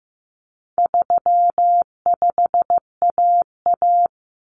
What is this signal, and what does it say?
Morse code, heard as a keyed tone, about 15 wpm, 700 Hz. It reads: 35AA
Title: Common Cut Numbers - 4-Digit 15wpm